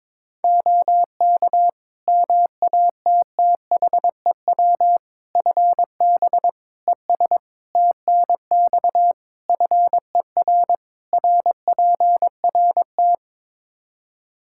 Morse code, heard as a keyed tone, about 21 wpm, 700 Hz. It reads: OK MATTHEW FB ES TNX FER RPRT
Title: Practice QSO - Exchange 2 - Info - 22wpm